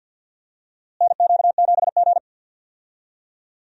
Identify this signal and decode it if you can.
Morse code, heard as a keyed tone, about 51 wpm, 700 Hz. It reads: NQ7G
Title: USA Call Signs - 4 Characters - Repeated 3x - 50wpm